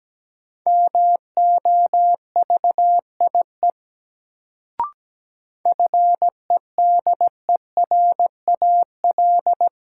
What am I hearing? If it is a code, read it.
Morse code, heard as a keyed tone, about 17 wpm, 700 Hz. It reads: MOVIE FEDERAL
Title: Sentences from Top 1000 Words - Review Part 5 - New Word Drill - 17wpm